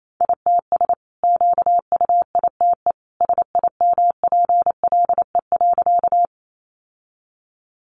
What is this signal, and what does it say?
Morse code, heard as a keyed tone, about 28 wpm, 700 Hz. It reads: ITS QUITE SIMPLE.
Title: Sentences from Top 700 Words 28wpm